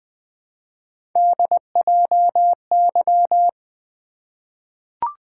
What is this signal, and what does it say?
Morse code, heard as a keyed tone, about 20 wpm, 700 Hz. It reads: DJY
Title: Three Letter-Number Combinations - 20wpm